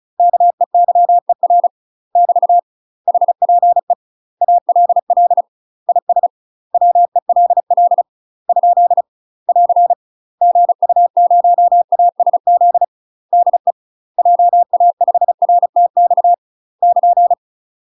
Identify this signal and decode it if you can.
Morse code, heard as a keyed tone, about 34 wpm, 700 Hz. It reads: KEYER <BT> HPE ALL IS WELL ? <AR> GU0ASZ DE JA5RTX <KN>
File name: UK MASTER QSOs TRAINING / 35WPM / UK MASTER QSOs 0004